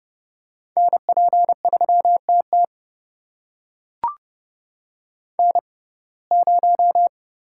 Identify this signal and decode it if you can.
Morse code, heard as a keyed tone, about 30 wpm, spaced as for 22 wpm, 700 Hz. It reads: NP3TT N0
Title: USA Call Signs - 5 Characters - 6x Character Spacing - 30wpm